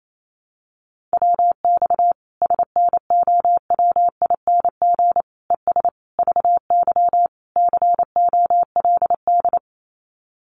Morse code, heard as a keyed tone, about 28 wpm, 700 Hz. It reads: WX SNOWING ES VY COLD
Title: Common QSO Phrases 28wpm